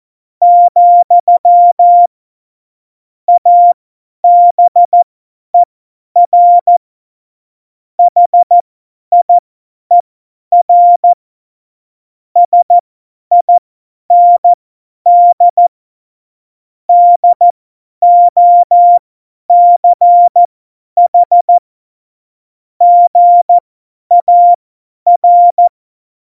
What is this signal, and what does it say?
Morse code, heard as a keyed tone, about 14 wpm, spaced as for 10 wpm, 700 Hz. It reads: , ABER HIER SIND DOCH GAR
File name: Morsekurs / Witze / Witz 09. Januar 2018, 50 F70